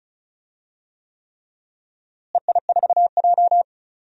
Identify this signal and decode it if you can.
Morse code, heard as a keyed tone, about 36 wpm, 700 Hz. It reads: EI4J
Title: DXCC-100 Call Signs - 4 Characters - Repeated 3x - 35wpm